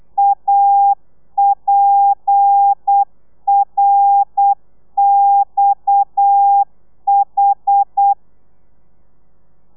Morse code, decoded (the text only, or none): APRXH